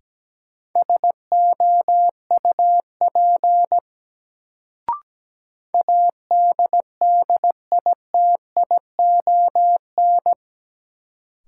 Morse code, heard as a keyed tone, about 17 wpm, 700 Hz. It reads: SOUP ADDITION
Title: Sentences from Top 1000 Words - Review Part 6 - New Word Drill - 17wpm